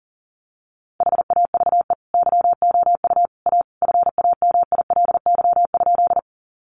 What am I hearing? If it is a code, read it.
Morse code, heard as a keyed tone, about 40 wpm, 700 Hz. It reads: HAVE YOU A FAMILY?
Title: Sentences from Top 400 Words 40wpm